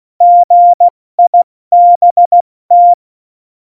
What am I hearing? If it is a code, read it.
Morse code, heard as a keyed tone, about 16 wpm, 700 Hz. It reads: GIBT
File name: Morsekurs / Witze / Witz 04. September 2018, 70 F80